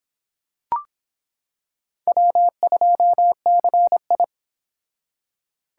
Morse code, heard as a keyed tone, about 25 wpm, 700 Hz. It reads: W2CI